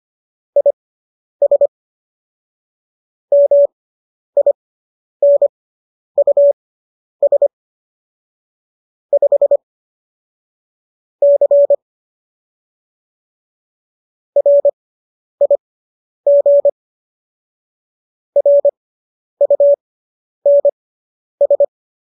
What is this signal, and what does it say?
Morse code, heard as a keyed tone, about 25 wpm, spaced as for 10 wpm, 570 Hz. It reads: IS MINUS 5 C RIG RUNS